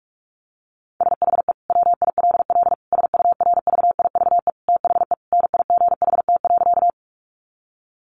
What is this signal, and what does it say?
Morse code, heard as a keyed tone, about 44 wpm, 700 Hz. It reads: SHE WILL SURVIVE THE NIGHT.
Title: Sentences from Top 1300 Words 45wpm